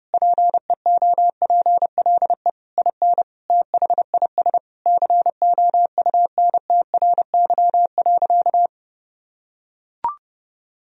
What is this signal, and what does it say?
Morse code, heard as a keyed tone, about 31 wpm, 700 Hz. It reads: PEOPLE IN THIS COUNTRY.
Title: Sentences from Top 1000 Words - Review Part 3 - 30wpm